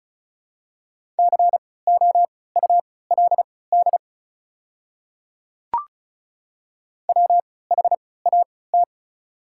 Morse code, heard as a keyed tone, about 36 wpm, spaced as for 20 wpm, 700 Hz. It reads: COULD WHAT
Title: 3x Character Spacing - Top 100 Words - 35wpm